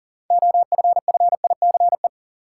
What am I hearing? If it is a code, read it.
Morse code, heard as a keyed tone, about 40 wpm, 700 Hz. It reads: OFFICE
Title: Sentences from Top 1000 Words - New Word Drill - 40wpm